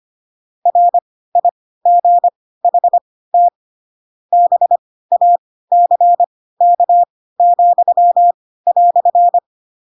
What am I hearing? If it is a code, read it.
Morse code, heard as a keyed tone, about 25 wpm, spaced as for 16 wpm, 700 Hz. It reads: RIGHT BACK,"